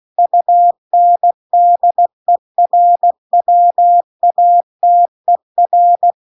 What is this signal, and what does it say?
Morse code, heard as a keyed tone, about 16 wpm, 700 Hz. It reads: UNDERWATER
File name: CW-The-Big-Twoheaded-River-2-16WPM